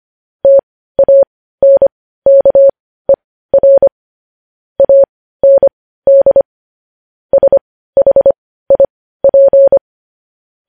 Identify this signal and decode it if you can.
Morse code, heard as a keyed tone, about 25 wpm, spaced as for 15 wpm, 550 Hz. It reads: TANKER AND SHIP
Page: news headlines in morse code at 15 wpm